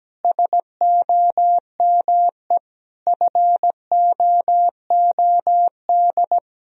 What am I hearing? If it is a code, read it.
Morse code, heard as a keyed tone, about 17 wpm, 700 Hz. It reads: SOME FOOD
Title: Sentences from Top 1000 Words - Review Part 1 - 17wpm